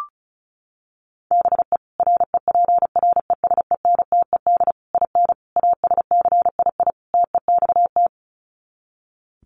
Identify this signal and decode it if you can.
Morse code, heard as a keyed tone, about 34 wpm, 700 Hz. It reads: BE REPRESENTED IN ASCII TEXT
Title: Sets of 5 Words 35wpm